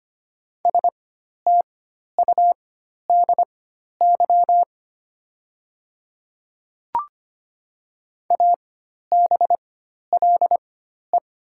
Morse code, heard as a keyed tone, about 25 wpm, spaced as for 12 wpm, 700 Hz. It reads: STUDY ABLE
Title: 4x Character Spacing - Top 500 Words - 25wpm